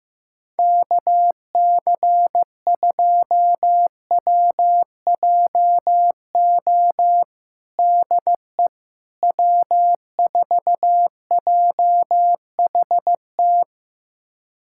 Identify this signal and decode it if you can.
Morse code, heard as a keyed tone, about 15 wpm, 700 Hz. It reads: KC2WJO DE W4JHT
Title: USA Callsign DE Callsign 15wpm